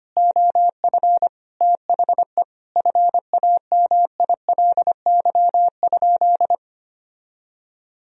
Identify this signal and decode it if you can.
Morse code, heard as a keyed tone, about 25 wpm, 700 Hz. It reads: OF THE FAMILY?
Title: Sentences from Top 1000 Words - Review Part 4 - 25wpm